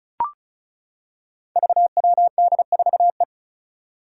Morse code, heard as a keyed tone, about 36 wpm, 700 Hz. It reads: VWD4E
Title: Five Challenging Letter-Number Combinations 35wpm